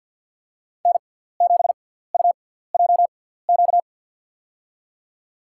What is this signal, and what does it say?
Morse code, heard as a keyed tone, about 53 wpm, spaced as for 17 wpm, 700 Hz. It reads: N7UJQ
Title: USA Call Signs - 5 Characters - 6x Character Spacing - 50wpm